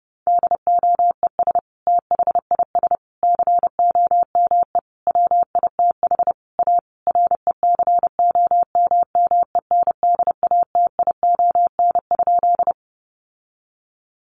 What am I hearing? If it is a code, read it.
Morse code, heard as a keyed tone, about 30 wpm, 700 Hz. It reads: DOES THIS COME WITH A RECOMMENDATION?
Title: Sentences from Top 1400 Words 30wpm